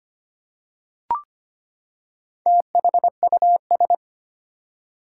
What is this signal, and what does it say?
Morse code, heard as a keyed tone, about 25 wpm, 700 Hz. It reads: THUS
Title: Four Letter Words - 25wpm